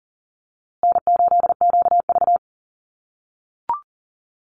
Morse code, heard as a keyed tone, about 40 wpm, 700 Hz. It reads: N8QV
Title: USA Call Signs Repeated 2x 40wpm